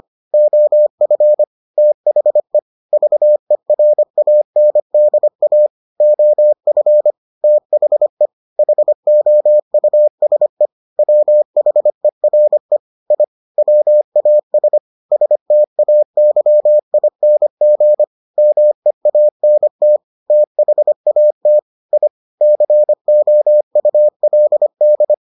Morse code, heard as a keyed tone, about 25 wpm, 600 Hz. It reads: OF THE VERANDA OF THE HOUSE WHERE I WAS STAYING MEANT THAT I COULD